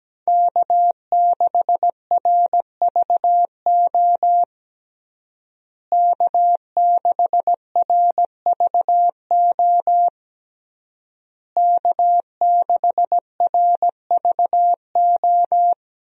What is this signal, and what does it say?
Morse code, heard as a keyed tone, about 17 wpm, 700 Hz. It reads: K6RVO K6RVO K6RVO
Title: USA Call Signs - 5 Characters - Repeated 3x - 17wpm